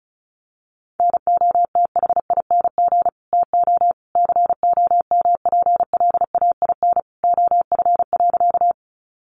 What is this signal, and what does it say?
Morse code, heard as a keyed tone, about 34 wpm, 700 Hz. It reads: NOTHING TO COMPLAIN OF.
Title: Sentences from Top 1300 Words 35wpm